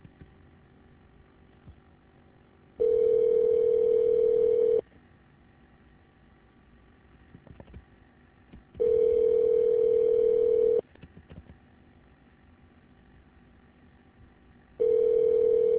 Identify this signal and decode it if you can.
Ringback tone (North America), 440 + 480 Hz.